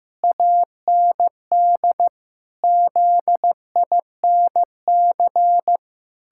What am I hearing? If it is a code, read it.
Morse code, heard as a keyed tone, about 15 wpm, 700 Hz. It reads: AND ZINC
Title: Pangram Challenge v2 - 15wpm